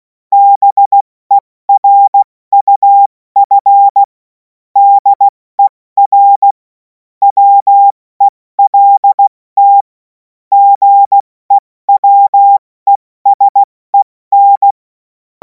Morse code, heard as a keyed tone, about 16 wpm, spaced as for 14 wpm, 800 Hz. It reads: BERUF DER WELT GEWESEN
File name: Morsekurs / Witze / Witz 24. Februar 2026 70 F80